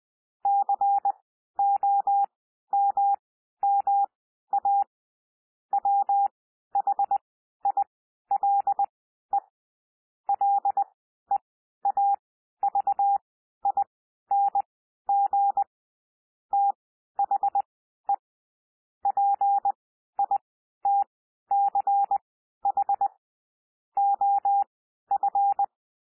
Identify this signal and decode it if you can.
Morse code, heard as a keyed tone, about 20 wpm, spaced as for 12 wpm, 800 Hz. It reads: COMMA WHILE LEAVING THE PITCH OF
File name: cwmp3 / P1 SOUND PASSAGE 20 wpm